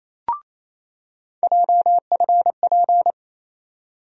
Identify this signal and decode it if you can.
Morse code, heard as a keyed tone, about 28 wpm, 700 Hz. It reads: JFP
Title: USA Call Sign Postfixes - 3 Letters 28wpm